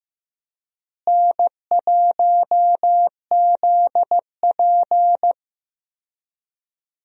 Morse code, heard as a keyed tone, about 15 wpm, 700 Hz. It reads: N1ZP
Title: USA Call Signs - 4 Characters - Repeated 3x - 15wpm